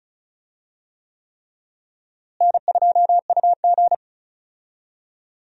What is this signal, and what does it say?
Morse code, heard as a keyed tone, about 36 wpm, 700 Hz. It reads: N2UG
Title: USA Call Signs - 4 Characters - Repeated 3x - 35wpm